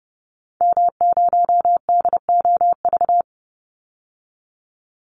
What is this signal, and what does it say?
Morse code, heard as a keyed tone, about 30 wpm, 700 Hz. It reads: M0DOV
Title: Call Signs Repeated 2x - 30wpm